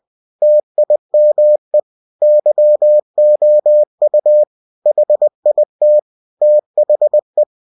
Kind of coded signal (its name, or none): Morse code